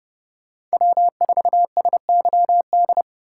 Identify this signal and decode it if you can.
Morse code, heard as a keyed tone, about 32 wpm, 700 Hz. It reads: W4SYD